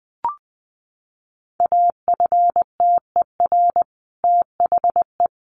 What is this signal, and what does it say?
Morse code, heard as a keyed tone, about 20 wpm, 700 Hz. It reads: AFTER THE
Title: A Sound Of Thunder 20wpm